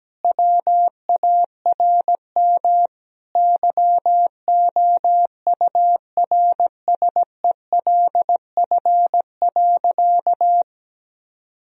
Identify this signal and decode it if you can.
Morse code, heard as a keyed tone, about 17 wpm, 700 Hz. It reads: WARM YOURSELF.
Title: Sentences from Top 1000 Words - Review Part 2 - 17wpm